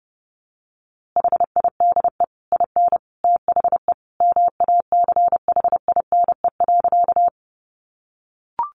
Morse code, heard as a keyed tone, about 30 wpm, 700 Hz. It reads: HIDE IN THE MACHINE.
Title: A Sound Of Thunder 30wpm